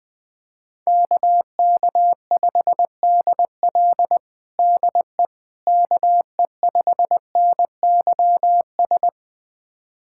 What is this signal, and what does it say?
Morse code, heard as a keyed tone, about 20 wpm, 700 Hz. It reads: KK5DL DE KE5NYS
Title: USA Callsign DE Callsign 20wpm